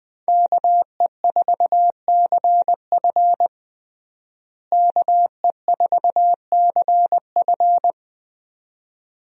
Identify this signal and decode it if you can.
Morse code, heard as a keyed tone, about 20 wpm, 700 Hz. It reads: KE4CF KE4CF
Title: USA Call Signs - 5 Characters - Repeated 2x - 20wpm